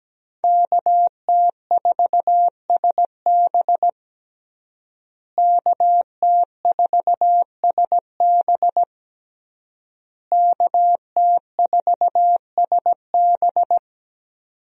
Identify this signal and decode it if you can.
Morse code, heard as a keyed tone, about 17 wpm, 700 Hz. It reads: KT4SB KT4SB KT4SB
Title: USA Call Signs - 5 Characters - Repeated 3x - 17wpm